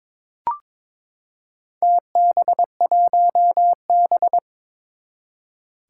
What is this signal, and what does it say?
Morse code, heard as a keyed tone, about 23 wpm, 700 Hz. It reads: TB1B
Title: Four Challenging Letter-Number Combinations 22wpm